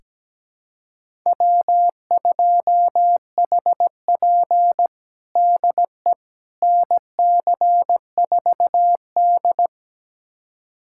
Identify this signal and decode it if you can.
Morse code, heard as a keyed tone, about 17 wpm, 700 Hz. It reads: W2HP DE NC4D
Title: USA Callsign DE Callsign - 4 Characters - 17wpm